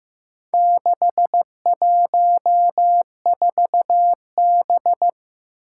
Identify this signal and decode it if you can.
Morse code, heard as a keyed tone, about 15 wpm, 700 Hz. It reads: 614B
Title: Four Challenging Letter-Number Combinations 15wpm